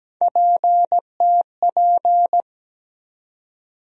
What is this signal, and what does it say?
Morse code, heard as a keyed tone, about 17 wpm, 700 Hz. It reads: PTP
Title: USA Call Sign Postfixes - 3 Letters 17wpm